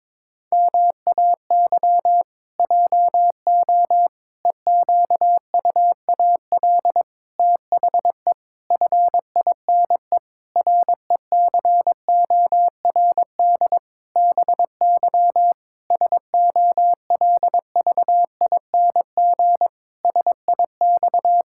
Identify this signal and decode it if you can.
Morse code, heard as a keyed tone, about 22 wpm, 700 Hz. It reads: MAY JO EQUAL THE FINE RECORD BY SOLVING SIX
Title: Pangram Challenge v2 - 22wpm